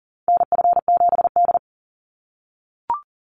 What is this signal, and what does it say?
Morse code, heard as a keyed tone, about 40 wpm, 700 Hz. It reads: NF7D
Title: Call Signs - 40wpm